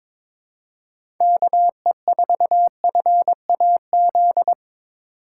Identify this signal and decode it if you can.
Morse code, heard as a keyed tone, about 22 wpm, 700 Hz. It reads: KE4FAZ
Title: USA Call Signs - 2x Character Spacing - 22wpm